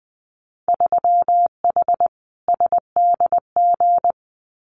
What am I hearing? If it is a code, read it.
Morse code, heard as a keyed tone, about 21 wpm, 700 Hz. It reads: 3H SDG
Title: Winter Field Day Exchanges Encore 20wpm